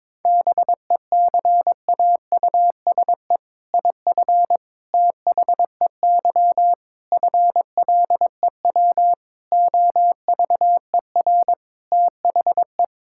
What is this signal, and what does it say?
Morse code, heard as a keyed tone, about 22 wpm, 700 Hz. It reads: BECAUSE IF THEY FLEW OVER THE
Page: morse-code-ninja-practice-22wpm